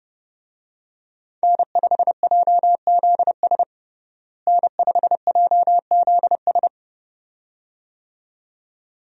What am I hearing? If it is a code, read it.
Morse code, heard as a keyed tone, about 30 wpm, 700 Hz. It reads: N5JZS N5JZS
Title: USA Call Signs - 5 Characters - Repeated 2x - 30wpm